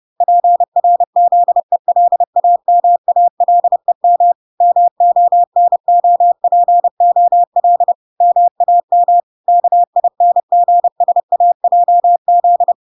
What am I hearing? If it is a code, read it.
Morse code, heard as a keyed tone, about 30 wpm, 700 Hz. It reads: PRZELAMALEM MONOPOL MAM KINGSAJZ